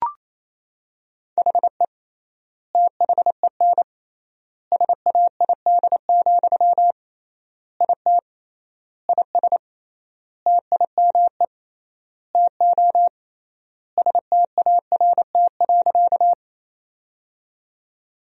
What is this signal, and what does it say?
Morse code, heard as a keyed tone, about 27 wpm, 700 Hz. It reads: HE THEN SAID, IT IS TIME TO START.
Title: 3x Word-Spacing - Sentences from Top 300 Words - 28wpm